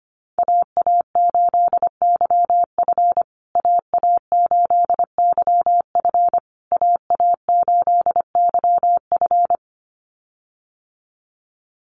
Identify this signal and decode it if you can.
Morse code, heard as a keyed tone, about 25 wpm, 700 Hz. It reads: AA8YF AA8YF AA8YF
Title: USA Call Signs Repeated 3x 25wpm